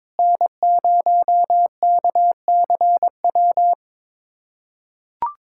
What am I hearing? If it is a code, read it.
Morse code, heard as a keyed tone, about 22 wpm, 700 Hz. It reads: N0KCW
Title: USA Callsign DE Callsign - 5 Characters - 22wpm